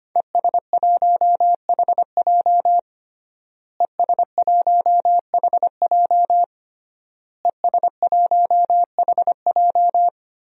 Morse code, heard as a keyed tone, about 25 wpm, 700 Hz. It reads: ES1HJ ES1HJ ES1HJ
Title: DXCC-100 Call Signs - 5 Characters - Repeated 3x - 25wpm